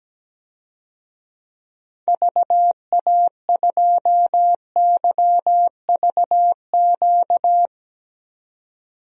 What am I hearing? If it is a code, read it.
Morse code, heard as a keyed tone, about 17 wpm, 700 Hz. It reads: VA2YVQ
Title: DXCC-100 Call Signs - 6 Characters - 17wpm